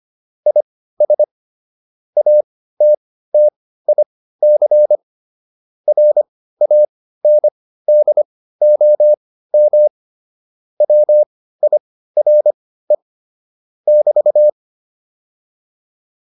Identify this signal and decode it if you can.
Morse code, heard as a keyed tone, about 25 wpm, spaced as for 15 wpm, 600 Hz. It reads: IS ATTIC RANDOM WIRE <BT>